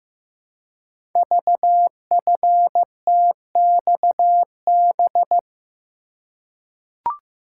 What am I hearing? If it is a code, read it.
Morse code, heard as a keyed tone, about 15 wpm, 700 Hz. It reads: VFTXB
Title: Code Groups - 5 Letters - 15wpm